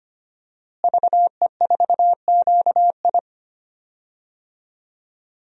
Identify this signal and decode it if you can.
Morse code, heard as a keyed tone, about 25 wpm, 700 Hz. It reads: VE4QI
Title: DXCC-100 Call Signs - 5 Characters - Repeated 3x - 25wpm